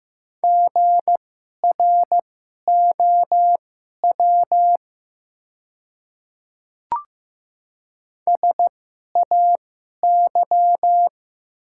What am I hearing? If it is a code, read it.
Morse code, heard as a keyed tone, about 15 wpm, spaced as for 11 wpm, 700 Hz. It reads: GROW SAY
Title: 2x Character Spacing - Top 300 Words - 15wpm